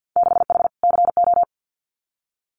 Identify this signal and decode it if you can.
Morse code, heard as a keyed tone, about 50 wpm, 700 Hz. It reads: 6H CO
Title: Winter Field Day Exchanges Encore 50wpm